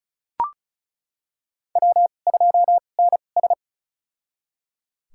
Morse code, heard as a keyed tone, about 36 wpm, spaced as for 25 wpm, 700 Hz. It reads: W2NS
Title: USA Call Signs - 4 Characters - 2x Character Spacing - 35wpm